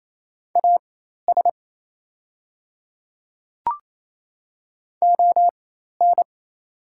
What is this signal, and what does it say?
Morse code, heard as a keyed tone, about 28 wpm, spaced as for 13 wpm, 700 Hz. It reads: AS ON
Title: 4x Character Spacing - Top 100 Words - 28wpm